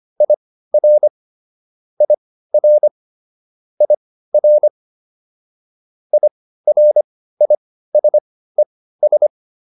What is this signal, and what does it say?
Morse code, heard as a keyed tone, about 25 wpm, spaced as for 15 wpm, 600 Hz. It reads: IR IR IR IRISES